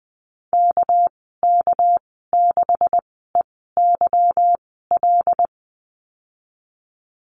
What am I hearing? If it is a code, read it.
Morse code, heard as a keyed tone, about 20 wpm, spaced as for 14 wpm, 700 Hz. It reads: KK6EYL
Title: 245 Lesson 39g - Letter z - Call Signs 20wpm